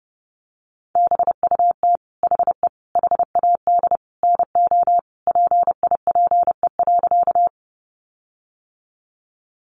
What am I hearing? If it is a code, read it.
Morse code, heard as a keyed tone, about 30 wpm, 700 Hz. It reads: BUT HE HAD NO PIPE.
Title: Sentences from Top 1500 Words 30wpm